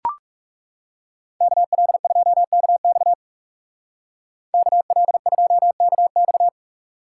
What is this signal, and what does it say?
Morse code, heard as a keyed tone, about 36 wpm, 700 Hz. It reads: KL2KX KL2KX
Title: Speed Racing - USA 2x2 Call Signs 30wpm